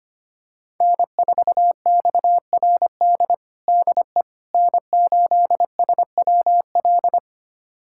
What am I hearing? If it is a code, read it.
Morse code, heard as a keyed tone, about 25 wpm, 700 Hz. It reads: N4XRD DE N8SWL
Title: USA Callsign DE Callsign 25wpm